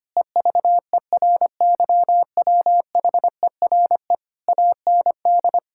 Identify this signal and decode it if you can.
Morse code, heard as a keyed tone, about 25 wpm, 700 Hz. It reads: EVERYWHERE AND
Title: Sentences from Top 1600 Words 25wpm